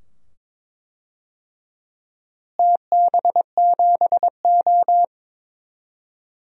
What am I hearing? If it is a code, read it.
Morse code, heard as a keyed tone, about 22 wpm, 700 Hz. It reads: TB7O
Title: DXCC-100 Call Signs - 4 Characters - Repeated 2x - 22wpm